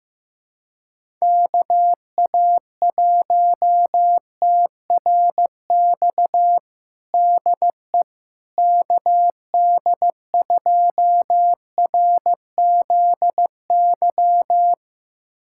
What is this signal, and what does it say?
Morse code, heard as a keyed tone, about 16 wpm, 700 Hz. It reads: KA1TRX DE KD2RZY
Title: USA Callsign DE Callsign - 6 Characters - 15wpm